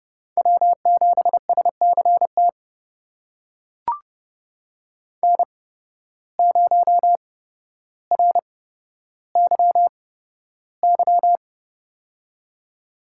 Morse code, heard as a keyed tone, about 30 wpm, spaced as for 13 wpm, 700 Hz. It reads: W7SCT N0RYY
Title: USA Call Signs - 5 Characters - 8x Character Spacing - 30wpm